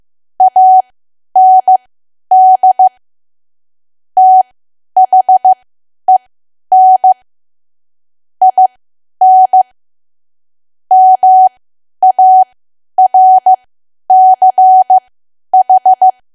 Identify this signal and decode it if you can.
Morse code, heard as a keyed tone, about 15 wpm, spaced as for 10 wpm, 750 Hz. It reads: AND THEN IN MARCH
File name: cwa / W1AW-Level2 / 10WPM